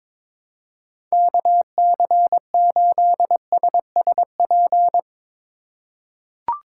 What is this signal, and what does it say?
Morse code, heard as a keyed tone, about 22 wpm, 700 Hz. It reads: KC8SSP
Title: USA Call Signs - 6 Characters - 4x Character Spacing - 22wpm